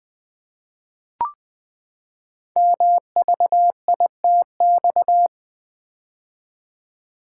Morse code, heard as a keyed tone, about 20 wpm, 700 Hz. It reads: MVITX
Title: Code Groups - 5 Letters - 20wpm